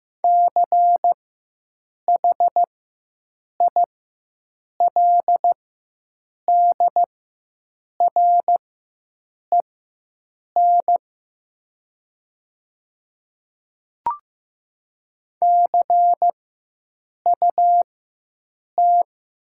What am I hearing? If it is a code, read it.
Morse code, heard as a keyed tone, about 15 wpm, spaced as for 7 wpm, 700 Hz. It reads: CHILDREN CUT
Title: 4x Character Spacing - Top 300 Words - 15wpm